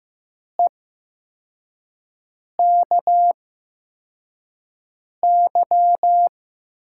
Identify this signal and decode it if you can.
Morse code, heard as a keyed tone, about 15 wpm, spaced as for 4 wpm, 700 Hz. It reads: EKY
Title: USA Call Signs - 6 Characters - 8x Character Spacing - 15wpm